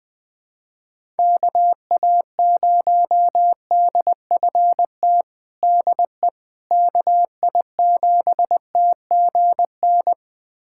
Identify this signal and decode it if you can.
Morse code, heard as a keyed tone, about 20 wpm, 700 Hz. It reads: KA0DFT DE KI7TGN
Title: USA Callsign DE Callsign - 6 Characters - 20wpm